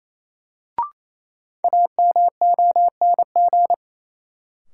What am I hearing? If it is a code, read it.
Morse code, heard as a keyed tone, about 28 wpm, 700 Hz. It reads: AMONG
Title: Sets of 2 Words - New Word Drill - 28wpm